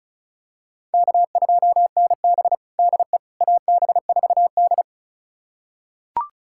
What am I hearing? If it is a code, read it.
Morse code, heard as a keyed tone, about 36 wpm, 700 Hz. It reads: K2NB DE AB4D
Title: USA Callsign DE Callsign - 4 Characters - 35wpm